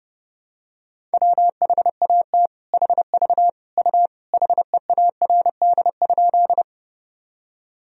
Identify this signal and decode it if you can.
Morse code, heard as a keyed tone, about 30 wpm, 700 Hz. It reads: WHAT HV U HEARD?
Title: Sentences from Top 400 Words and Abbreviations 30wpm